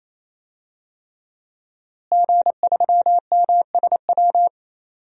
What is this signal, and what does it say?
Morse code, heard as a keyed tone, about 28 wpm, 700 Hz. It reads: G3MSW
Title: DXCC-100 Call Signs - 5 Characters - Repeated 3x - 28wpm